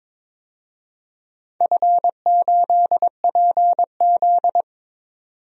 Morse code, heard as a keyed tone, about 22 wpm, 700 Hz. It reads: F8PZ
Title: DXCC-100 Call Signs - 4 Characters - 22wpm